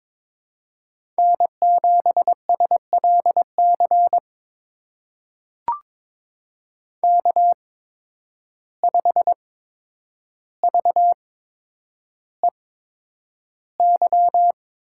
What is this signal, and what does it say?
Morse code, heard as a keyed tone, about 21 wpm, spaced as for 9 wpm, 700 Hz. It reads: N7SLC K5VEY
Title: USA Call Signs - 5 Characters - 8x Character Spacing - 22wpm